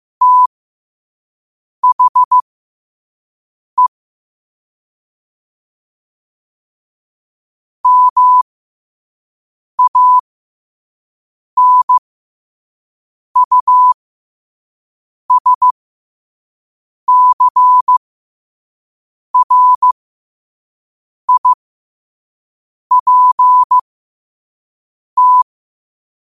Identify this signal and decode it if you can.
Morse code, heard as a keyed tone, about 15 wpm, spaced as for 5 wpm, 1000 Hz. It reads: THE MANUSCRIPT
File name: qotdfiles / 5wpm